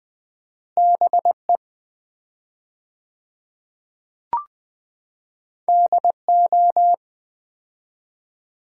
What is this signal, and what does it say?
Morse code, heard as a keyed tone, about 20 wpm, 700 Hz. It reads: BE DO
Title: 4x Word-Spacing - Sets of 4 Words - 20wpm